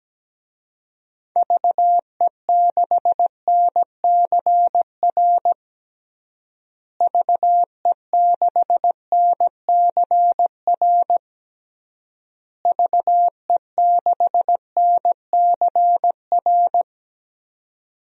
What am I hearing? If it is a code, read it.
Morse code, heard as a keyed tone, about 17 wpm, 700 Hz. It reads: VE6NCR VE6NCR VE6NCR
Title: DXCC-100 Call Signs - 6 Characters - Repeated 3x - 17wpm